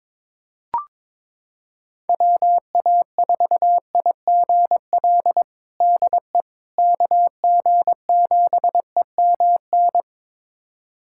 Morse code, heard as a keyed tone, about 21 wpm, 700 Hz. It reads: WA4IGL DE KG7EMN
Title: USA Callsign DE Callsign - 6 Characters - 22wpm